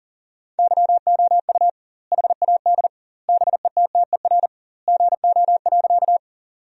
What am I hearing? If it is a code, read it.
Morse code, heard as a keyed tone, about 40 wpm, 700 Hz. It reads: YOU HAD BETTER GO.
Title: 2x Word-Spacing - Sentences from Top 400 Words - 40wpm